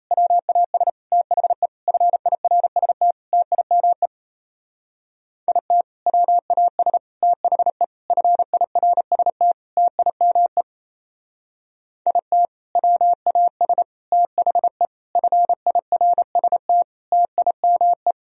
Morse code, heard as a keyed tone, about 36 wpm, 700 Hz. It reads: WAS THE FIRST TIME IT WAS THE FIRST TIME IT WAS THE FIRST TIME